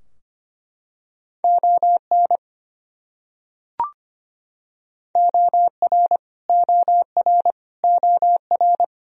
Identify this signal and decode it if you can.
Morse code, heard as a keyed tone, about 25 wpm, 700 Hz. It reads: ON OR OR OR
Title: Top 100 Words Repeated 25wpm